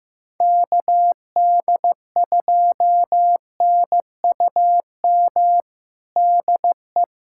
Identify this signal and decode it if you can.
Morse code, heard as a keyed tone, about 15 wpm, 700 Hz. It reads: KD2NUM DE
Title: USA Callsign DE Callsign 15wpm